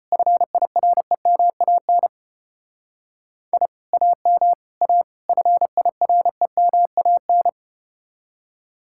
Morse code, heard as a keyed tone, about 34 wpm, 700 Hz. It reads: FIREMAN I AM A FIREMAN